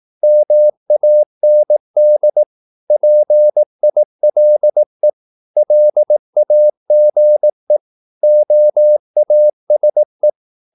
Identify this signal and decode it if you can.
Morse code, heard as a keyed tone, about 18 wpm, 600 Hz. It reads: MAND PILE LAGE OASE
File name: Korte ord DK 18wpm